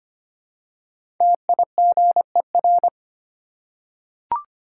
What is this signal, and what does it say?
Morse code, heard as a keyed tone, about 25 wpm, 700 Hz. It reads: TIGER
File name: practice / 25wpm / Easy Words with Hints - 25wpm